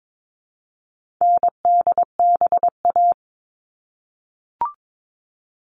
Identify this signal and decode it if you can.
Morse code, heard as a keyed tone, about 22 wpm, 700 Hz. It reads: NDBA
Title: All Cut Numbers - 4-Digit 22wpm